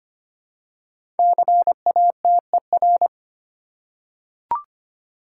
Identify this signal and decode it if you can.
Morse code, heard as a keyed tone, about 25 wpm, 700 Hz. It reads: CATER
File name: 5 Letter Words - First 3 Letters Given - 25wpm